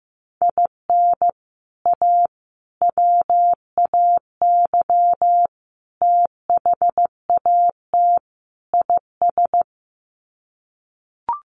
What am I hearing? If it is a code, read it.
Morse code, heard as a keyed tone, about 15 wpm, 700 Hz. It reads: IN A WAY THAT IS
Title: Sets of 5 Words Encore 15wpm